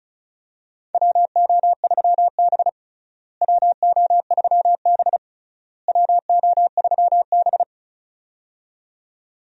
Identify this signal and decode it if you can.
Morse code, heard as a keyed tone, about 36 wpm, 700 Hz. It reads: WO3B WO3B WO3B
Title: USA Call Signs - 4 Characters - Repeated 3x - 35wpm